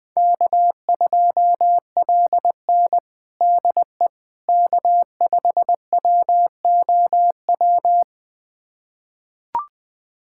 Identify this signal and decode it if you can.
Morse code, heard as a keyed tone, about 21 wpm, 700 Hz. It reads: K2LN DE K5WOW
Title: USA Callsign DE Callsign 20wpm